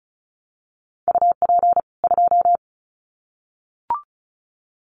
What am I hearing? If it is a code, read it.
Morse code, heard as a keyed tone, about 34 wpm, 700 Hz. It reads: UP 2
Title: Common QSO Phrases 35wpm